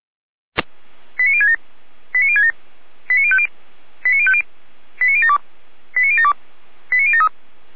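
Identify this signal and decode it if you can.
Five-tone selective call (ZVEI), heard as quick runs of five tones, tones of about 80 ms. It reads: 89067 89067 89050 89050 89072 89072 89073
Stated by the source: ZVEI five-tone selective call 89022 received by SDR, from Feuerwehr Orlamünde.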